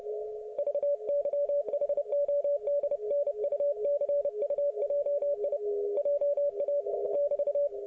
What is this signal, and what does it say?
Morse code, heard as a keyed tone, about 30 wpm, 570 Hz. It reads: VYHODNUCUJI JA <BT>
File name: ok1fhi